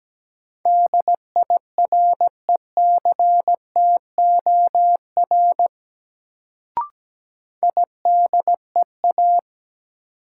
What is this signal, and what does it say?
Morse code, heard as a keyed tone, about 17 wpm, 700 Hz. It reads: DIRECTOR IDEA
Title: Sets of 3 Words - New Word Drill - 17wpm